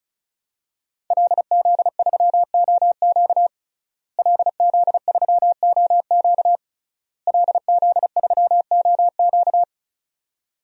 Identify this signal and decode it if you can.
Morse code, heard as a keyed tone, about 36 wpm, 700 Hz. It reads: LZ3OQ LZ3OQ LZ3OQ
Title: DXCC-100 Call Signs - 5 Characters - Repeated 3x - 35wpm